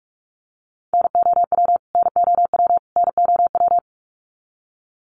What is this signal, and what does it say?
Morse code, heard as a keyed tone, about 44 wpm, 700 Hz. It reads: NOW NOW NOW
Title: Top 500 Words Repeated 45wpm